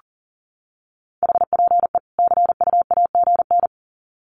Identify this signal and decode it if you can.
Morse code, heard as a keyed tone, about 40 wpm, 700 Hz. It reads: HPE CUAGN
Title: Common QSO Phrases 40wpm